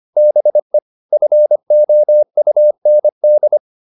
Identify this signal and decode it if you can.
Morse code, heard as a keyed tone, about 25 wpm, 600 Hz. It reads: BE FOUND